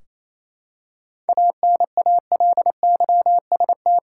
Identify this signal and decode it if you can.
Morse code, heard as a keyed tone, about 28 wpm, 700 Hz. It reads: ANALYST
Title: Practice QSO - Exchange 3 - More Info - 28wpm